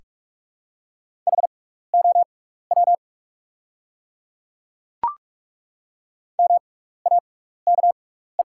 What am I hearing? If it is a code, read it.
Morse code, heard as a keyed tone, about 43 wpm, spaced as for 15 wpm, 700 Hz. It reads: HOW MAKE
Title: 6x Character Spacing - Top 100 Words - 45wpm